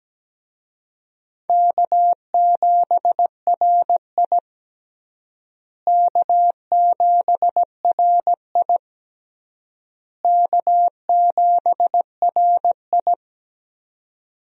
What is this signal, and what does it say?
Morse code, heard as a keyed tone, about 17 wpm, 700 Hz. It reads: K7RI K7RI K7RI
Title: USA Call Signs - 4 Characters - Repeated 3x - 17wpm